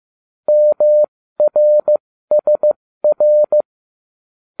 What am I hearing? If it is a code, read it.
Morse code, heard as a keyed tone, about 15 wpm, spaced as for 13 wpm, 600 Hz. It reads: MRSR